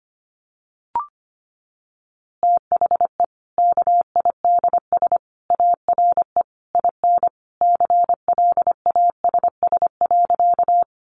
Morse code, heard as a keyed tone, about 25 wpm, 700 Hz. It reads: THE KIDS ARE IN CLASS.
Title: Easy Phrases 25wpm